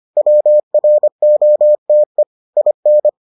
Morse code, heard as a keyed tone, about 25 wpm, 600 Hz. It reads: WROTE IN